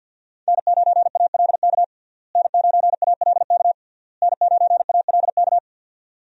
Morse code, heard as a keyed tone, about 46 wpm, 700 Hz. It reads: N9ALK N9ALK N9ALK
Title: USA Call Signs - 5 Characters - Repeated 3x - 50wpm